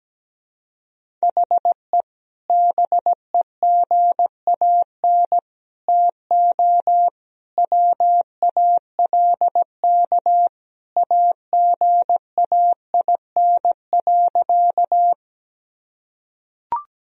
Morse code, heard as a keyed tone, about 17 wpm, 700 Hz. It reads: HE BEGAN TO WALK AGAIN.
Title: Sentences from Top 1000 Words - Review Part 1 - 17wpm